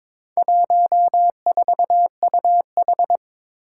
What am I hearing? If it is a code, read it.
Morse code, heard as a keyed tone, about 22 wpm, 700 Hz. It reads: 14UH
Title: Four Challenging Letter-Number Combinations 22wpm